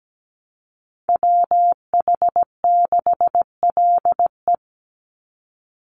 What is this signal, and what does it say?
Morse code, heard as a keyed tone, about 17 wpm, 700 Hz. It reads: WH6LE
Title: NA SOTA Activator Callsigns 17wpm